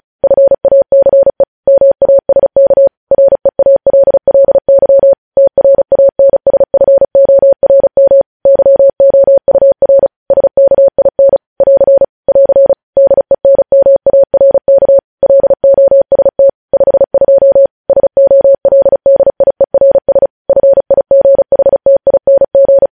Morse code, heard as a keyed tone, about 36 wpm, 550 Hz. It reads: FACE MASK REALLY TRANSFORM YOUR SKIN <AR> <AR> DENMARK LOST 52 SOLDIERS FIGHTING
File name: podcast / News headlines-1x-35x35